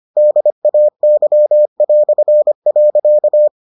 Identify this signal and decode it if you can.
Morse code, heard as a keyed tone, about 25 wpm, 600 Hz. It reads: DAY".